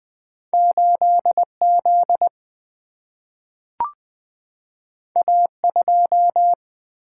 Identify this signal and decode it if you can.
Morse code, heard as a keyed tone, about 20 wpm, 700 Hz. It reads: 8Z A2
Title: Single-Letter Single-Number 20wpm